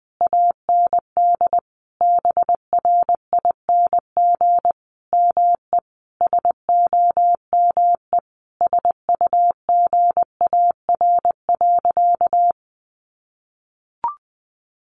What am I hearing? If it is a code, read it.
Morse code, heard as a keyed tone, about 20 wpm, 700 Hz. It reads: AND BRING ME SOME SUGAR.
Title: Sentences from Top 1200 Words 20wpm